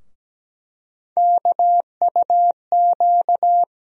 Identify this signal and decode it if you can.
Morse code, heard as a keyed tone, about 17 wpm, 700 Hz. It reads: KUQ
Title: USA Call Sign Postfixes - 3 Letters 17wpm